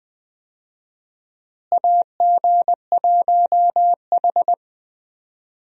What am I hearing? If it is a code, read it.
Morse code, heard as a keyed tone, about 21 wpm, 700 Hz. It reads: AG1H
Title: USA Call Signs - 4 Characters - Repeated 2x - 20wpm